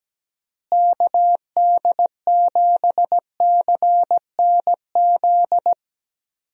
Morse code, heard as a keyed tone, about 17 wpm, 700 Hz. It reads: KD7CNZ
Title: USA Call Signs - 6 Characters - Repeated 2x - 17wpm